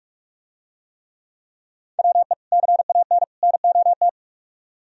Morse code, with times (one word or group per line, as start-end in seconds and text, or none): 2.0-2.3 WE
2.5-3.2 CAN
3.4-4.1 NOT